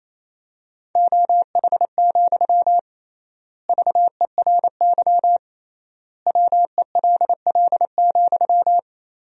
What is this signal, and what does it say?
Morse code, heard as a keyed tone, about 28 wpm, 700 Hz. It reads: OH, VERY WELL,
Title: 3x Word-Spacing - Sentences from Top 500 Words - 28wpm